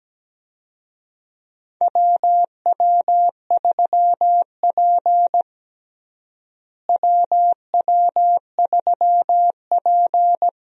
Morse code, heard as a keyed tone, about 17 wpm, 700 Hz. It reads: WW3P WW3P
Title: USA Call Signs - 4 Characters - Repeated 3x - 17wpm